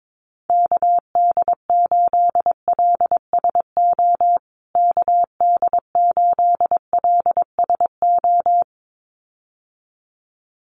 Morse code, heard as a keyed tone, about 22 wpm, 700 Hz. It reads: KD8LSO KD8LSO
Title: USA Call Signs Repeated 2x 22wpm